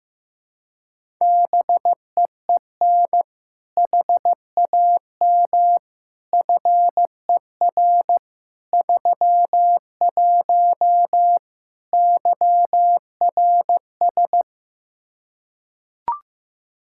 Morse code, heard as a keyed tone, about 16 wpm, 700 Hz. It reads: BEEN HAM FER 31 YRS
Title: K1USN SST QSO Practice 15wpm